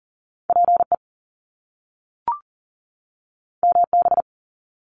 Morse code, heard as a keyed tone, about 40 wpm, 700 Hz. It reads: PE MB
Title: Canadian Provinces and Territories 40wpm